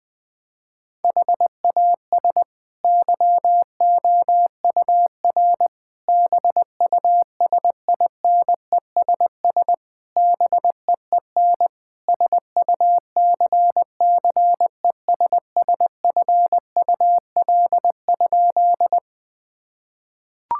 Morse code, heard as a keyed tone, about 20 wpm, 700 Hz. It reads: HAS YOUR BUSINESS BEEN SUCCESSFUL?
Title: Sentences from Top 1000 Words - Review Part 6 - 20wpm